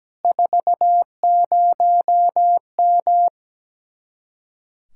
Morse code, heard as a keyed tone, about 17 wpm, 700 Hz. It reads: 40M
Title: Speed-Racing Common QSO Phrases - 17wpm